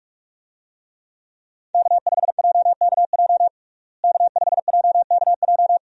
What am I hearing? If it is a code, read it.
Morse code, heard as a keyed tone, about 45 wpm, 700 Hz. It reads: K5JKJ K5JKJ
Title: USA Call Signs - 5 Characters - Repeated 3x - 45wpm